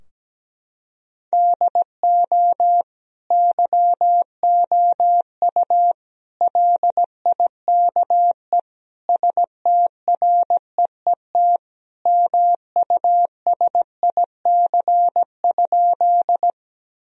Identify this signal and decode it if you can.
Morse code, heard as a keyed tone, about 17 wpm, 700 Hz. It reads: DO YOU LIKE STREET MUSIC?
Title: Sentences from Top 1000 Words - Review Part 3 - 17wpm